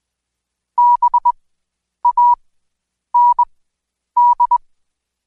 Morse code, heard as a keyed tone, about 21 wpm, spaced as for 9 wpm, 970 Hz. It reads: BAND
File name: Code Quick / AudioDownld2 / Seg115-Day20-DayDrill 20